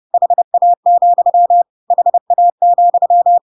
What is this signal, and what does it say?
Morse code, heard as a keyed tone, about 30 wpm, 700 Hz. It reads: HA, HA,
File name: Pinocchio / Pinocchio ch1 30 30 ik1hge